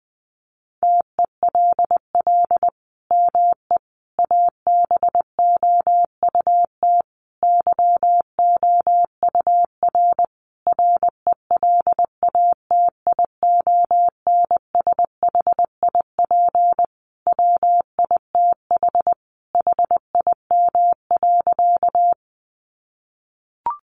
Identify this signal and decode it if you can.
Morse code, heard as a keyed tone, about 20 wpm, 700 Hz. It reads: TELL ME ABOUT YOUR RELATIONSHIP WITH HIM.
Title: Sentences from Top 1100 Words 20wpm